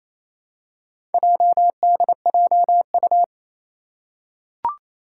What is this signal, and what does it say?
Morse code, heard as a keyed tone, about 29 wpm, 700 Hz. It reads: JDJU
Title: Four Challenging Letter-Number Combinations 28wpm